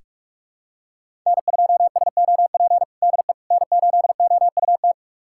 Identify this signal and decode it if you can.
Morse code, heard as a keyed tone, about 46 wpm, 700 Hz. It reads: N2SOP DE N8OUT
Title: USA Callsign DE Callsign - 5 Characters - 45wpm